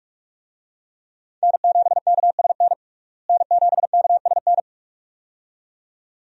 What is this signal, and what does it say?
Morse code, heard as a keyed tone, about 47 wpm, 700 Hz. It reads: N7KSN N7KSN
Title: USA Call Signs - 5 Characters - Repeated 2x - 45wpm